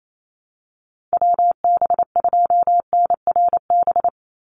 Morse code, heard as a keyed tone, about 28 wpm, 700 Hz. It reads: WB2NRB
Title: USA Call Signs 28wpm